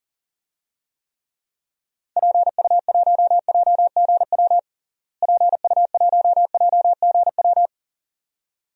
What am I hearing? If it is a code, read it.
Morse code, heard as a keyed tone, about 37 wpm, 700 Hz. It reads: PU1JGW PU1JGW
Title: DXCC-100 Call Signs - 6 Characters - Repeated 2x - 40wpm